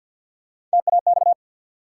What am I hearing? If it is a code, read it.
Morse code, heard as a keyed tone, about 54 wpm, 700 Hz. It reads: TAX
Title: Three Letter Words - 50wpm